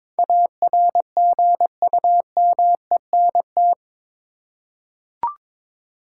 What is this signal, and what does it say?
Morse code, heard as a keyed tone, about 22 wpm, 700 Hz. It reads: ARGUMENT
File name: practice / 22wpm / Eight Letter Words - 22wpm